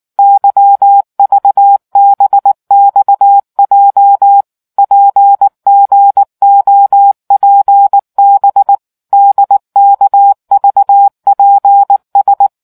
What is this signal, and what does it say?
Morse code, heard as a keyed tone, about 19 wpm, 800 Hz. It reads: YVBXJ PGOPB DKVPS